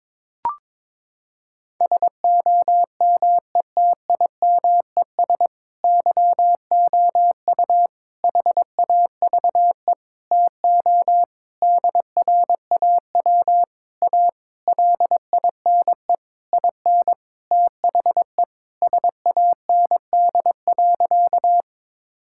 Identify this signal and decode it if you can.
Morse code, heard as a keyed tone, about 22 wpm, 700 Hz. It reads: SOMETIMES YOU HAVE TO DRAW A LINE IN THE SAND.
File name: Sentences with 7-Word Phrases - 22wpm